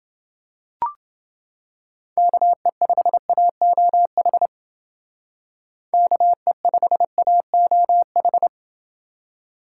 Morse code, heard as a keyed tone, about 29 wpm, 700 Hz. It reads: KE5AOH KE5AOH